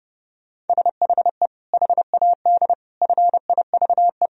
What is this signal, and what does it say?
Morse code, heard as a keyed tone, about 32 wpm, 700 Hz. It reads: SHE HAD FIVE
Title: Sentences from Top 1000 Words - Review Part 2 - 30wpm